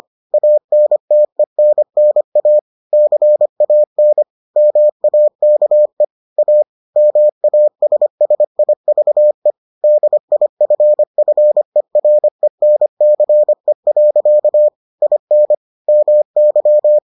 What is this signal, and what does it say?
Morse code, heard as a keyed tone, about 25 wpm, 600 Hz. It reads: ANTENNA CAN MAKE A MASSIVE DIFFERENCE. IN MY